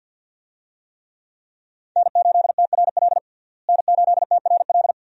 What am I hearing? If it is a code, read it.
Morse code, heard as a keyed tone, about 52 wpm, 700 Hz. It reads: N8TRL N8TRL
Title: USA Call Signs - 5 Characters - Repeated 2x - 50wpm